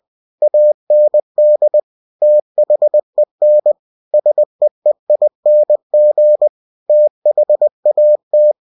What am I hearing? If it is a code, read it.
Morse code, heard as a keyed tone, about 20 wpm, 600 Hz. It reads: AND THEN SEEING THAT
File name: cw-books / winnie-the-pooh / winnie-20wpm-page-0102